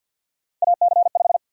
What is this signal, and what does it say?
Morse code, heard as a keyed tone, about 52 wpm, 700 Hz. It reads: AC5
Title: USA Call Sign Prefixes - 50wpm